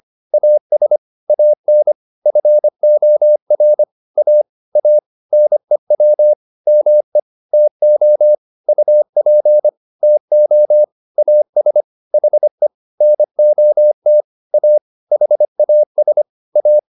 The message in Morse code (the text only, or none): AS AN FOR A A NEW ME TO UP TO AS HE NOT A HAS A